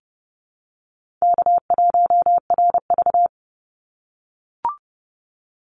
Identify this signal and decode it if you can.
Morse code, heard as a keyed tone, about 30 wpm, 700 Hz. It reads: K1RV
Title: Easy Call Signs - 30wpm